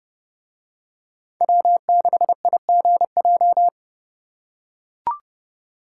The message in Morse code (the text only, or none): W6IGJ